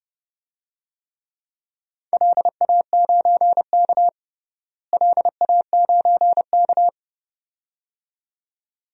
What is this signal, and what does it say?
Morse code, heard as a keyed tone, about 30 wpm, 700 Hz. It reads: LA9K LA9K
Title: DXCC-100 Call Signs - 4 Characters - Repeated 2x - 30wpm